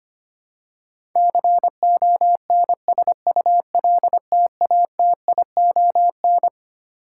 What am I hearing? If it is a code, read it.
Morse code, heard as a keyed tone, about 25 wpm, 700 Hz. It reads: CONSULTATION
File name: Twelve Letter Words - 25wpm